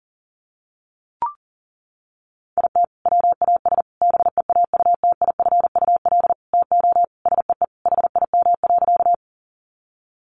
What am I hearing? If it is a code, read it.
Morse code, heard as a keyed tone, about 40 wpm, 700 Hz. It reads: IT WAS BEAUTIFUL TO SEE HIM.
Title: Sentences from Top 900 Words 40wpm